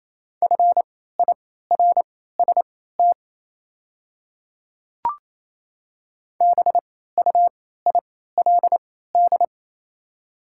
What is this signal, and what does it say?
Morse code, heard as a keyed tone, about 28 wpm, spaced as for 16 wpm, 700 Hz. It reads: FIRST BUILD
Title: 3x Character Spacing - Top 200 Words - 28wpm